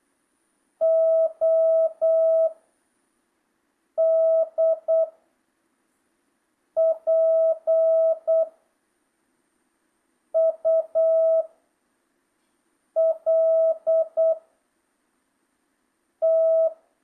Morse code, heard as a keyed tone, about 8 wpm, 1300 Hz. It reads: O D P U L T
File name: jh8cbh / 2025cw